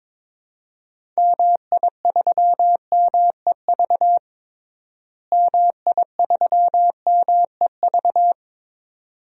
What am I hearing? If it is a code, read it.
Morse code, heard as a keyed tone, about 22 wpm, 700 Hz. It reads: MI3MEV MI3MEV
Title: DXCC-100 Call Signs - 6 Characters - Repeated 2x - 22wpm